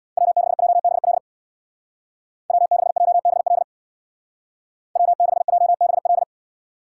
Morse code, heard as a keyed tone, about 64 wpm, 700 Hz. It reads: W6JBL W6JBL W6JBL